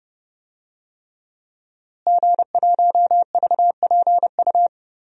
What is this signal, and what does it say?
Morse code, heard as a keyed tone, about 29 wpm, 700 Hz. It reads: G1VPU
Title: DXCC-100 Call Signs - 5 Characters - Repeated 3x - 30wpm